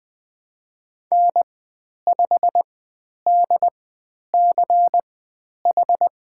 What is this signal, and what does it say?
Morse code, heard as a keyed tone, about 20 wpm, spaced as for 10 wpm, 700 Hz. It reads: N5DCH
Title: 102 Lesson 17g - Letter u - Call Signs 20wpm